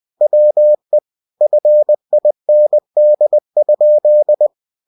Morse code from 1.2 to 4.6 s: FIND?